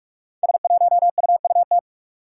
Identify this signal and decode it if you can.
Morse code, heard as a keyed tone, about 44 wpm, 700 Hz. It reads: S1UUT
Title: Five Challenging Letter-Number Combinations 45wpm